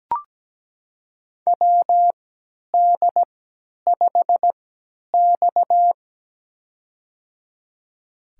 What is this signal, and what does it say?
Morse code, heard as a keyed tone, about 17 wpm, spaced as for 10 wpm, 700 Hz. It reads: WD5X
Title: USA Call Signs - 4 Characters - 3x Character Spacing - 17wpm